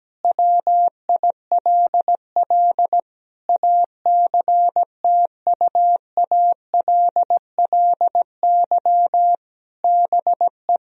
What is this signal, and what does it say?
Morse code, heard as a keyed tone, about 17 wpm, 700 Hz. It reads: WILL ACTUALLY BE
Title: Sentences from Top 1000 Words - Review Part 4 - 17wpm